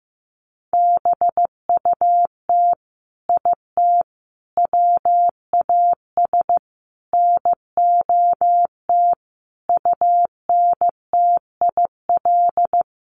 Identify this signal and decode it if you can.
Morse code, heard as a keyed tone, about 15 wpm, 700 Hz. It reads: BUT IT WAS NOT UNTIL
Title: Sets of 5 Words Encore2 15wpm